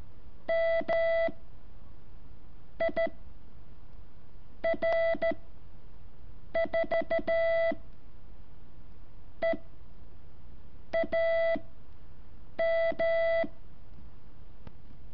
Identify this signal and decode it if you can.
Morse code, heard as a keyed tone, about 13 wpm, 660 Hz. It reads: M I R 4 E A M
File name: curso / ejercicio 7